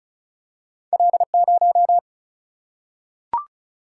Morse code, heard as a keyed tone, about 36 wpm, 700 Hz. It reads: L0
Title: Single-Letter Single-Number 35wpm